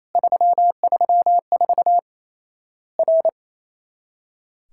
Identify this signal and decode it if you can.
Morse code, heard as a keyed tone, about 28 wpm, 700 Hz. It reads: 334 R
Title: Worked All Europe DX Contest - QTC Exchange - 28wpm